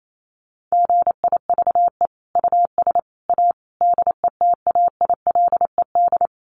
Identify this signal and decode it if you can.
Morse code, heard as a keyed tone, about 28 wpm, 700 Hz. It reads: GIVE US A DETAILED
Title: Sentences from Top 1500 Words 28wpm